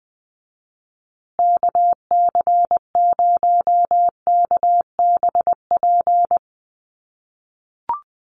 Morse code, heard as a keyed tone, about 20 wpm, 700 Hz. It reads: KC0KBP
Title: USA Call Signs Repeated 3x 20wpm